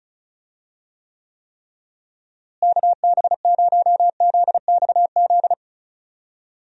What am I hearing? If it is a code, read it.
Morse code, heard as a keyed tone, about 36 wpm, 700 Hz. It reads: KB0ZXZ
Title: USA Call Signs - 6 Characters - Repeated 3x - 35wpm